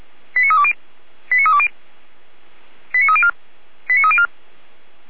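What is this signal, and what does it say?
Five-tone selective call (ZVEI), heard as quick runs of five tones, tones of about 70 ms. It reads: 89320 89320 89304 89304
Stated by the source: Five-tone ZVEI selective call with code 89276, received on an SDR by feuerwehr-orlamuende (fire brigade).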